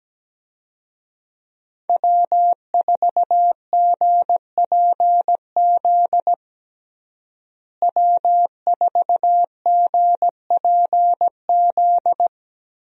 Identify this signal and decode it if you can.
Morse code, heard as a keyed tone, about 17 wpm, 700 Hz. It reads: W4GPZ W4GPZ
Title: USA Call Signs - 5 Characters - Repeated 3x - 17wpm